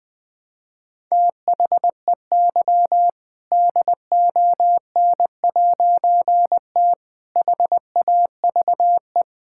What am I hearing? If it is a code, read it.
Morse code, heard as a keyed tone, about 21 wpm, 700 Hz. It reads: THEY DON'T HAVE
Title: All-Age Appropriate Jokes - 20wpm